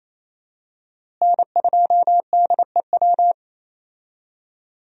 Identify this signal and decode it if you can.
Morse code, heard as a keyed tone, about 28 wpm, 700 Hz. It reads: N2DEW
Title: USA Call Signs - 5 Characters - Repeated 3x - 28wpm